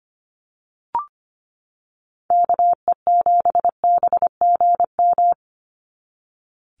Morse code, heard as a keyed tone, about 25 wpm, 700 Hz. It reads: KE7BGM
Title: NA SOTA Activator Callsigns 25wpm